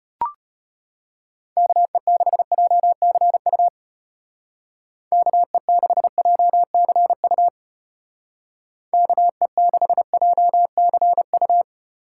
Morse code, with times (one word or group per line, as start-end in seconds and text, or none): 1.6-3.7 KE6JCU
5.1-7.5 KE6JCU
8.9-11.6 KE6JCU